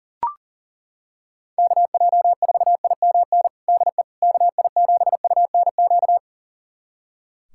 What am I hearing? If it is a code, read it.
Morse code, heard as a keyed tone, about 40 wpm, 700 Hz. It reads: KJ4IMN DE KI7UNQ
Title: USA Callsign DE Callsign - 6 Characters - 40wpm